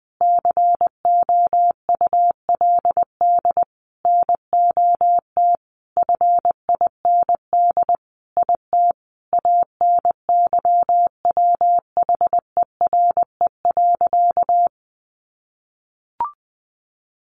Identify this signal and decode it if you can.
Morse code, heard as a keyed tone, about 20 wpm, 700 Hz. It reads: COULD NOT FIND IT ANYWHERE.
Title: Sentences from Top 1400 Words 20wpm